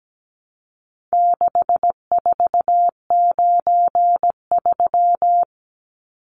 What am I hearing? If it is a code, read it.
Morse code, heard as a keyed tone, about 17 wpm, 700 Hz. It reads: 6493
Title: Numbers - 4-Digit 17wpm